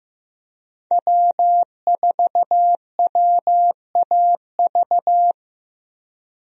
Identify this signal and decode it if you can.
Morse code, heard as a keyed tone, about 15 wpm, 700 Hz. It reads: W4WAV
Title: USA Call Signs - 5 Characters - Repeated 2x - 15wpm